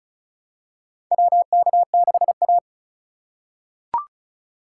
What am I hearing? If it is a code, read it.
Morse code, heard as a keyed tone, about 36 wpm, 700 Hz. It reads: WK6A
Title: USA Call Signs - 4 Characters - 4x Character Spacing - 35wpm